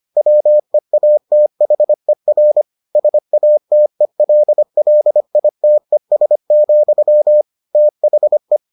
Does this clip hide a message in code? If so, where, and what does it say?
Yes: Morse code from 0.2 to 8.6 s, heard as a keyed tone, about 25 wpm, 600 Hz. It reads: WEATHER SATELLITES, THE